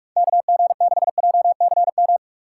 Morse code, heard as a keyed tone, about 58 wpm, 700 Hz. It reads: KG6JCM